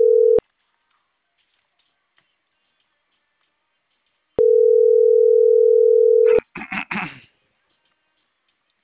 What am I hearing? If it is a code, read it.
Ringback tone (North America), 440 + 480 Hz.